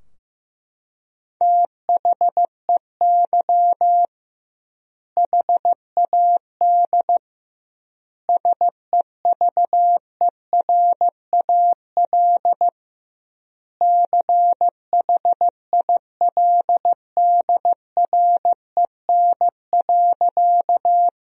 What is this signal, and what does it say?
Morse code, heard as a keyed tone, about 15 wpm, 700 Hz. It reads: THEY HAD SEVERAL CHILDREN.
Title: 2x Word-Spacing - Sentences from Top 400 Words - 15wpm